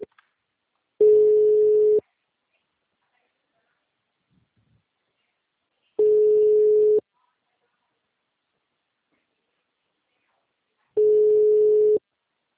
Ringback tone (Europe, 425 Hz).
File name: out- -2002- - - .8311